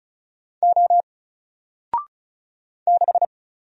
Morse code, heard as a keyed tone, about 36 wpm, 700 Hz. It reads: O 6
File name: Single-Letter-Number-35wpm